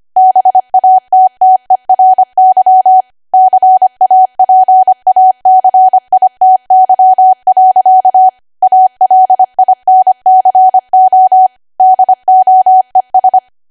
Morse code, heard as a keyed tone, about 25 wpm, 750 Hz. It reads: BATTERY CAPACITY. ALINCO DOES
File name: Archive / 25 WPM / 25WPM